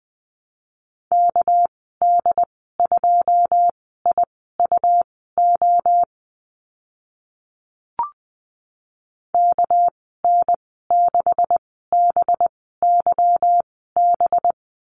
Morse code, heard as a keyed tone, about 20 wpm, spaced as for 15 wpm, 700 Hz. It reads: KD2IUO KN6BYB
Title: 213 Lesson 34g - Number 8 - Call Signs 20wpm